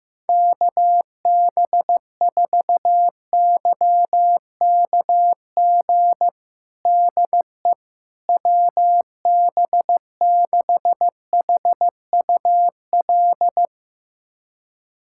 Morse code, heard as a keyed tone, about 15 wpm, 700 Hz. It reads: KB4YKG DE WB6HUL
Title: USA Callsign DE Callsign 15wpm